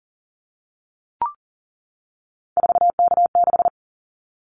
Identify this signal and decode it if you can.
Morse code, heard as a keyed tone, about 38 wpm, 700 Hz. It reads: 4K6
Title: Callsign Prefixes for Top 100 DXCC - 40wpm